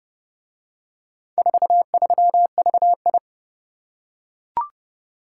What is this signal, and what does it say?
Morse code, heard as a keyed tone, about 30 wpm, 700 Hz. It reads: 43VI